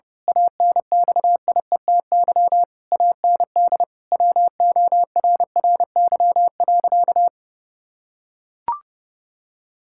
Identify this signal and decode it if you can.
Morse code, heard as a keyed tone, about 30 wpm, 700 Hz. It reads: ANXIETY AND WORRY.
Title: Pangram Challenge v2 - 30wpm